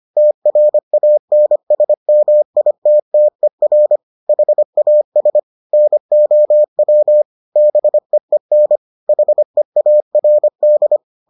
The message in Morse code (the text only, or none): TRANSMITTER HAS NOW BEEN HEARD